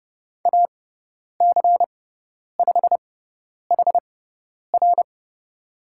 Morse code, heard as a keyed tone, about 30 wpm, spaced as for 10 wpm, 700 Hz. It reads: AC5HR
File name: Lesson 18g - Letter m - Call Signs 30wpm